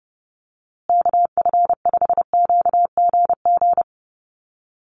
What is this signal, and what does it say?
Morse code, heard as a keyed tone, about 30 wpm, 700 Hz. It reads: KF5QGG
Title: USA Call Signs Repeated 2x 30wpm